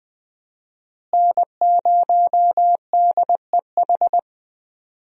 Morse code, heard as a keyed tone, about 19 wpm, 700 Hz. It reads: N0DEH
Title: USA Call Signs - 5 Characters - Repeated 3x - 20wpm